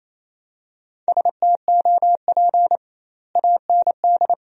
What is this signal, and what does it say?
Morse code, heard as a keyed tone, about 28 wpm, 700 Hz. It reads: STOP AND
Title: 2x Word-Spacing - Sentences from Top 300 Words - 28wpm